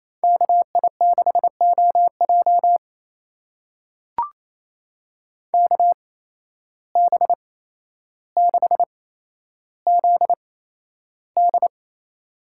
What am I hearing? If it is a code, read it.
Morse code, heard as a keyed tone, about 28 wpm, spaced as for 12 wpm, 700 Hz. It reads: KI6OJ KB6ZD
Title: USA Call Signs - 5 Characters - 8x Character Spacing - 28wpm